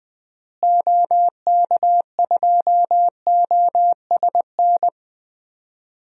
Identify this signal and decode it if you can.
Morse code, heard as a keyed tone, about 20 wpm, 700 Hz. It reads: OK2OSN
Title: DXCC-100 Call Signs - 6 Characters - Repeated 3x - 20wpm